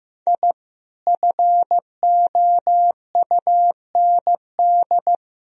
Morse code, heard as a keyed tone, about 15 wpm, 700 Hz. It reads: I FOUND
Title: Sentences from Top 1000 Words - Review Part 1 - 15wpm